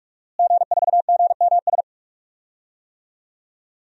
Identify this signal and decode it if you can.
Morse code, heard as a keyed tone, about 44 wpm, 700 Hz. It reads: G4GMS
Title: DXCC-100 Call Signs - 5 Characters - Repeated 3x - 45wpm